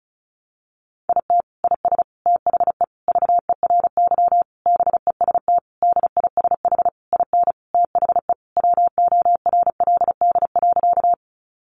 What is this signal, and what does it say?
Morse code, heard as a keyed tone, about 34 wpm, 700 Hz. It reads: IT IS THE VERY BEST DISH IN THE WORLD.
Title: Sentences from Top 1300 Words 35wpm